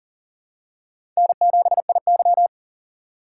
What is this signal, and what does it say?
Morse code, heard as a keyed tone, about 40 wpm, 700 Hz. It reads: N7IY
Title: USA Call Signs - 4 Characters - 8x Character Spacing - 40wpm